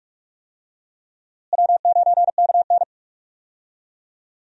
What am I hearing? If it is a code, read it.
Morse code, heard as a keyed tone, about 41 wpm, 700 Hz. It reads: W9KN
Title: USA Call Signs - 4 Characters - Repeated 2x - 45wpm